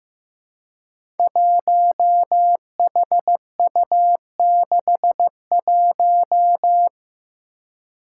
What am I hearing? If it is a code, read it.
Morse code, heard as a keyed tone, about 15 wpm, 700 Hz. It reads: 1HU61
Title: Five Challenging Letter-Number Combinations 15wpm